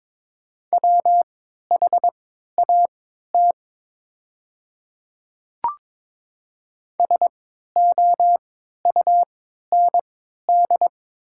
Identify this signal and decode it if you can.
Morse code, heard as a keyed tone, about 22 wpm, spaced as for 13 wpm, 690 Hz. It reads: WHAT SOUND
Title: 3x Character Spacing - Top 100 Words - 22wpm